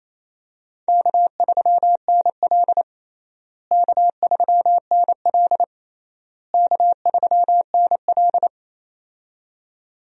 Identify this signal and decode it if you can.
Morse code, heard as a keyed tone, about 28 wpm, 700 Hz. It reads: K3NL K3NL K3NL
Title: USA Call Signs - 4 Characters - Repeated 3x - 28wpm